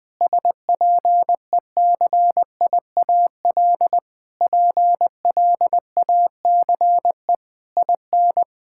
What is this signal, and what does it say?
Morse code, heard as a keyed tone, about 20 wpm, 700 Hz. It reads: SPECIAL PLACE IN
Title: Sets of 9 Words - 20wpm